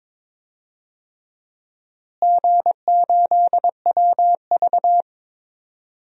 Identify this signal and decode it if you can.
Morse code, heard as a keyed tone, about 21 wpm, 700 Hz. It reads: G8WV
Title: DXCC-100 Call Signs - 4 Characters - Repeated 3x - 22wpm